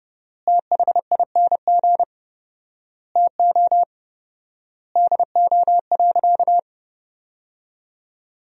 Morse code, heard as a keyed tone, about 32 wpm, 700 Hz. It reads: THING TO DO.
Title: 4x Word-Spacing - Sentences from Top 100 Words - 30wpm